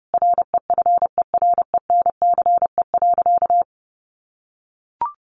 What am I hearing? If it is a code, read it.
Morse code, heard as a keyed tone, about 30 wpm, 700 Hz. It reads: REFERENCE.
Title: Sentences from Top 1200 Words 30wpm